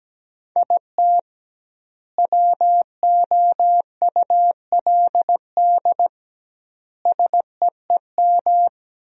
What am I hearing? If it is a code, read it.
Morse code, heard as a keyed tone, about 17 wpm, 700 Hz. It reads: IT WOULD SEEM
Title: 2x Word-Spacing - Sentences from Top 300 Words - 17wpm